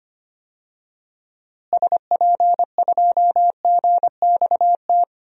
Morse code, heard as a keyed tone, about 25 wpm, 700 Hz. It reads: SP2GXT
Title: DXCC-100 Call Signs - 6 Characters - Repeated 2x - 25wpm